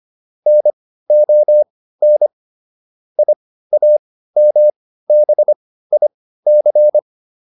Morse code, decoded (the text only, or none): NON IAMBIC